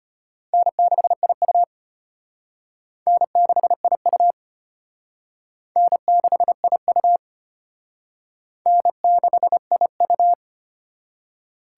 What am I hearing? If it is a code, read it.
Morse code, heard as a keyed tone, about 31 wpm, 700 Hz. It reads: N6IU N6IU N6IU N6IU
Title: Speed Racing - USA 1x2 Call Signs 25wpm